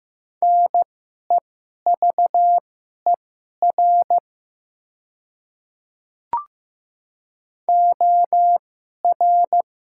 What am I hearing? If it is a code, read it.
Morse code, heard as a keyed tone, about 15 wpm, spaced as for 11 wpm, 690 Hz. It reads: NEVER OR
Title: 2x Character Spacing - Top 500 Words - 15wpm